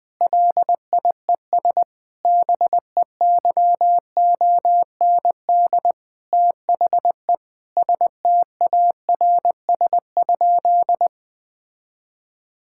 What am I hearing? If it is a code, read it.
Morse code, heard as a keyed tone, about 20 wpm, 700 Hz. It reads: LIES BEYOND THE STARS?
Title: Difficult Sentences with Hints - 20wpm